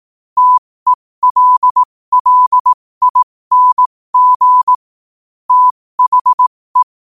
Morse code, heard as a keyed tone, about 18 wpm, spaced as for 15 wpm, 1000 Hz. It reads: TELLING THE
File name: qotdfiles / 15wpm